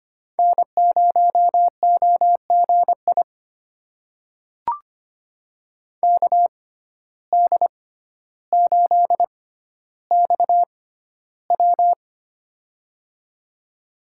Morse code, heard as a keyed tone, about 25 wpm, spaced as for 13 wpm, 700 Hz. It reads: N0OGI KD8XW
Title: USA Call Signs - 5 Characters - 6x Character Spacing - 25wpm